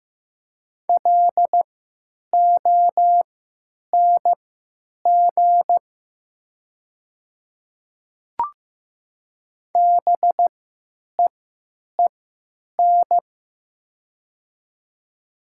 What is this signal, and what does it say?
Morse code, heard as a keyed tone, about 15 wpm, spaced as for 9 wpm, 700 Hz. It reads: LONG BEEN
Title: 3x Character Spacing - Top 200 Words - 15wpm